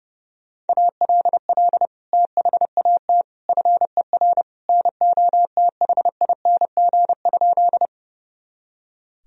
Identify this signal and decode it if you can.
Morse code, heard as a keyed tone, about 30 wpm, 700 Hz. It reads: ALL THAT FER NOTHING?
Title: Sentences from Top 500 Words and Abbreviations 30wpm